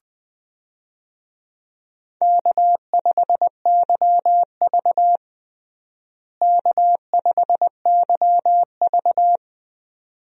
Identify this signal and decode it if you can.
Morse code, heard as a keyed tone, about 20 wpm, 700 Hz. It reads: K5YV K5YV
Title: USA Call Signs - 4 Characters - Repeated 2x - 20wpm